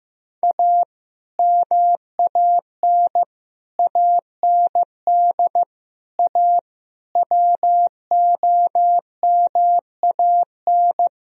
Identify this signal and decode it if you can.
Morse code, heard as a keyed tone, about 15 wpm, 700 Hz. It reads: A MAN AND A WOMAN
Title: Speed-Racing - Sets of 5 Words 15wpm